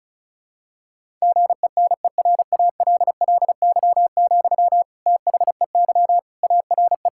Morse code, heard as a keyed tone, about 34 wpm, 700 Hz. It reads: GENERALLY, THEY ARE
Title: Sentences from Top 1000 Words - Review Part 6 - 35wpm